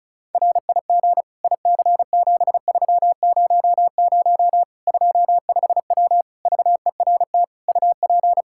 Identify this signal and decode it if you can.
Morse code, heard as a keyed tone, about 36 wpm, 700 Hz. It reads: RIG IC7300 25W VERT UP
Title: Practice QSO - Exchange 2 - Info - 35wpm